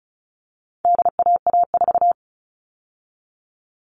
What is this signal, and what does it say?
Morse code, heard as a keyed tone, about 34 wpm, 700 Hz. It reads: DAA4
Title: All Cut Numbers - 4-Digit 35wpm